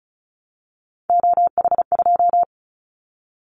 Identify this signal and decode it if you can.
Morse code, heard as a keyed tone, about 35 wpm, 700 Hz. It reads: OH2
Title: Callsign Prefixes for Top 50 DXCC - 35wpm